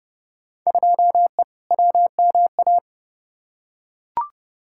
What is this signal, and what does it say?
Morse code, heard as a keyed tone, about 30 wpm, 700 Hz. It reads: 2E WMA
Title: Field Day Exchanges Encore - 30wpm